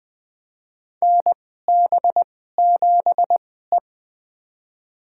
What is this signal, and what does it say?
Morse code, heard as a keyed tone, about 20 wpm, spaced as for 15 wpm, 700 Hz. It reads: NB7E
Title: USA Call Signs - 4 Characters - 2x Character Spacing - 20wpm